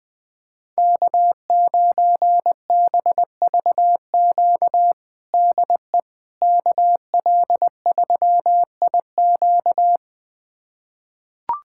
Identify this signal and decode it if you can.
Morse code, heard as a keyed tone, about 21 wpm, 700 Hz. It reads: K9BVQ DE KL3IQ
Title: USA Callsign DE Callsign - 5 Characters - 20wpm